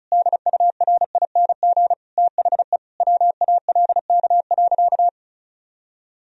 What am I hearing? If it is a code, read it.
Morse code, heard as a keyed tone, about 36 wpm, 700 Hz. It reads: DURING THE WALK.
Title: Sentences from Top 1000 Words - Review Part 2 - 35wpm